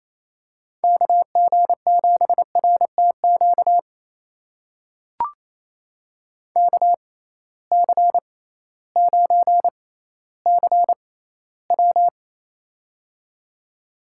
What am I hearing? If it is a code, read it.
Morse code, heard as a keyed tone, about 28 wpm, spaced as for 15 wpm, 700 Hz. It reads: KG7RTQ KC9CW
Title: USA Call Signs - 6x Character Spacing - 28wpm